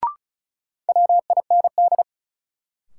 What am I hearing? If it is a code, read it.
Morse code, heard as a keyed tone, about 36 wpm, 700 Hz. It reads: WIND
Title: Sentences from Top 400 Words - New Word Drill - 35wpm